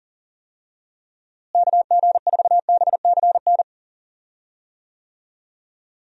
Morse code, heard as a keyed tone, about 40 wpm, 700 Hz. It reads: KG4BCN
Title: USA Call Signs - 6 Characters - Repeated 3x - 40wpm